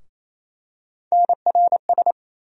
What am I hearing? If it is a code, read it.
Morse code, heard as a keyed tone, about 28 wpm, 700 Hz. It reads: NRS
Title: USA Call Sign Postfixes - 3 Letters 28wpm